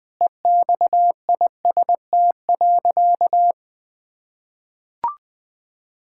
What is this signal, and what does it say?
Morse code, heard as a keyed tone, about 21 wpm, 700 Hz. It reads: EXIST.
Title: Sentences from Top 1000 Words - Review Part 5 - 20wpm